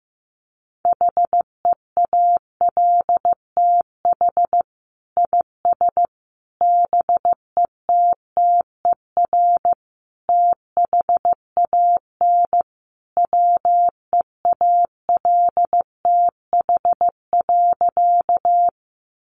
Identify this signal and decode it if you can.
Morse code, heard as a keyed tone, about 15 wpm, 700 Hz. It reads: HEALTH IS BETTER THAN WEALTH.
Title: Sentences from Top 1200 Words 15wpm